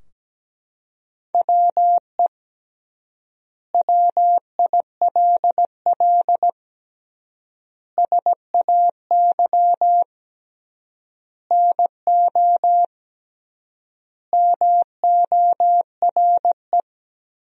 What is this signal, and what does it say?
Morse code, heard as a keyed tone, about 17 wpm, 700 Hz. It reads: WE WILL SAY NO MORE
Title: 3x Word-Spacing - Sentences from Top 200 Words - 17wpm